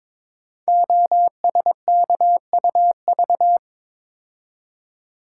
Morse code, heard as a keyed tone, about 22 wpm, 700 Hz. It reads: OSKUV
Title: Code Groups - 5 Letters - 22wpm